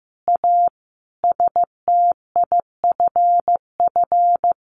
Morse code, heard as a keyed tone, about 15 wpm, 700 Hz. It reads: A STIFF
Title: A Sound Of Thunder 15wpm